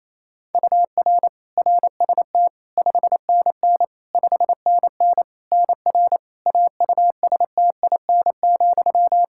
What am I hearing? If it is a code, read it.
Morse code, heard as a keyed tone, about 28 wpm, 700 Hz. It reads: UR RST 5NN 5NN NR AUSTIN,
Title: Practice QSO - Exchange 1 - Beginning - 28wpm